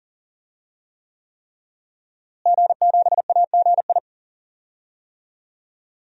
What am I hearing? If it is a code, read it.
Morse code, heard as a keyed tone, about 38 wpm, 700 Hz. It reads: G7AGI
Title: DXCC-100 Call Signs - 5 Characters - Repeated 3x - 40wpm